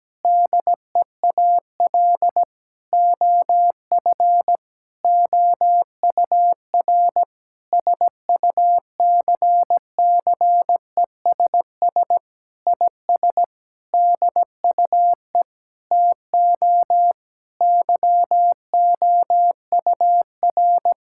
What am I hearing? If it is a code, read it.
Morse code, heard as a keyed tone, about 17 wpm, 700 Hz. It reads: DEAL OF OUR SUCCESS IS DUE TO YOUR
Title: Sentences from Top 1000 Words - Review Part 6 - 17wpm